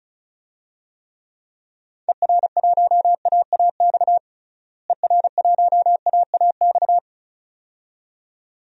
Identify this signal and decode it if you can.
Morse code, heard as a keyed tone, about 36 wpm, 700 Hz. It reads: ER1AAX ER1AAX
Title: DXCC-100 Call Signs - 6 Characters - Repeated 2x - 35wpm